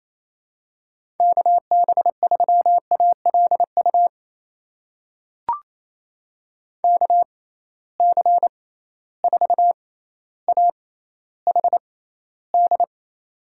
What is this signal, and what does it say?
Morse code, heard as a keyed tone, about 28 wpm, spaced as for 14 wpm, 700 Hz. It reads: KB3ALU KC4AHD
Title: USA Call Signs - 6x Character Spacing - 28wpm